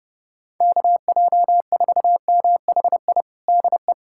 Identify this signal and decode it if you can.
Morse code, heard as a keyed tone, about 30 wpm, 700 Hz. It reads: KJ4MHI DE
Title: USA Callsign DE Callsign - 6 Characters - 30wpm